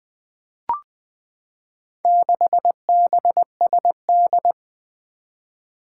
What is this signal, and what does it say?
Morse code, heard as a keyed tone, about 21 wpm, 700 Hz. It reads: 6BSD
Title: Four Challenging Letter-Number Combinations 20wpm